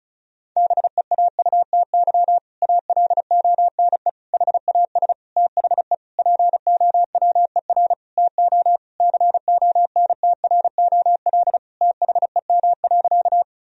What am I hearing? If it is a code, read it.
Morse code, heard as a keyed tone, about 34 wpm, 700 Hz. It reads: BEAUTY ALONE HAS THE POWER TO CONTROL THEM.
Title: Sentences from Top 1000 Words - Review Part 2 - 35wpm